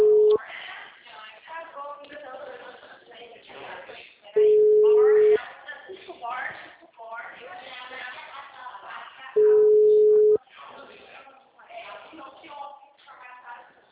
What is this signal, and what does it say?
Ringback tone (Europe, 425 Hz), with speech.
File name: out- -2001- - - .36990